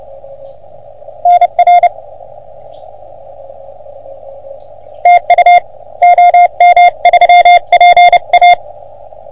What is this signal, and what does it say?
Morse code, heard as a keyed tone, about 30 wpm, 710 Hz. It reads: NR TU OM3PA
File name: OM3PA PILUP CUC 388